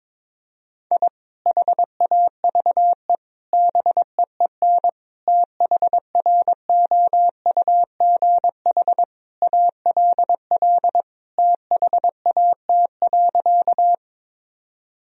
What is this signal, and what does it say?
Morse code, heard as a keyed tone, about 22 wpm, 700 Hz. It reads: I HAVE BEEN THROUGH ALL THAT.
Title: Sentences from Top 1000 Words - Review Part 1 - 22wpm